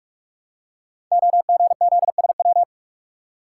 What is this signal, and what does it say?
Morse code, heard as a keyed tone, about 44 wpm, 700 Hz. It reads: OGZSW
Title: Code Groups - 5 Letter:Numbers - 45wpm